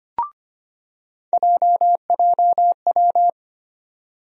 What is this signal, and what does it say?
Morse code, heard as a keyed tone, about 25 wpm, 700 Hz. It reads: JJW
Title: USA Call Sign Postfixes - 3 Letters 25wpm